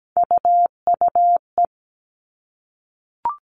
Morse code, heard as a keyed tone, about 17 wpm, 700 Hz. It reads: UUE
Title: All Cut Numbers - 3-Digit 17wpm